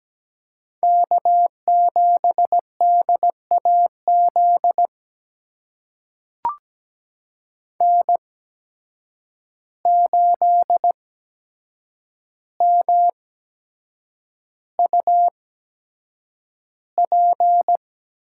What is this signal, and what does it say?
Morse code, heard as a keyed tone, about 17 wpm, spaced as for 7 wpm, 700 Hz. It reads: K7DAZ N8MUP
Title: USA Call Signs - 5 Characters - 8x Character Spacing - 17wpm